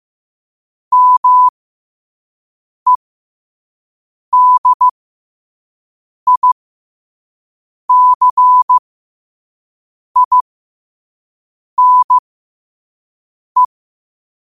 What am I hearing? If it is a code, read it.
Morse code, heard as a keyed tone, about 15 wpm, spaced as for 5 wpm, 1000 Hz. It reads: MEDICINE